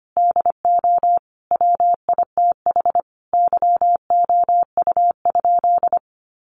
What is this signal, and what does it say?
Morse code, heard as a keyed tone, about 25 wpm, 700 Hz. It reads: DO WITH YOU?
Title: Sentences from Top 100 Words 25wpm